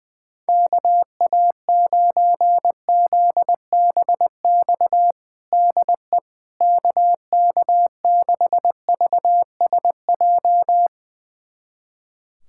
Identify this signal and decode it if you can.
Morse code, heard as a keyed tone, about 20 wpm, 700 Hz. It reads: KA9ZBX DE KK6VSJ
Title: USA Callsign DE Callsign - 6 Characters - 20wpm